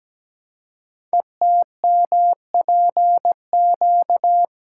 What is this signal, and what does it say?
Morse code, heard as a keyed tone, about 17 wpm, 700 Hz. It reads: ETMPQ
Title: Code Groups - 5 Letters - 17wpm